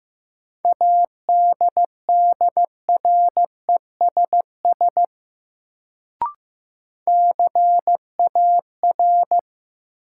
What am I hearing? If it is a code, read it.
Morse code, heard as a keyed tone, about 16 wpm, 700 Hz. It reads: ADDRESS CAR
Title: Sets of 5 Words Encore - New Word Drill - 15wpm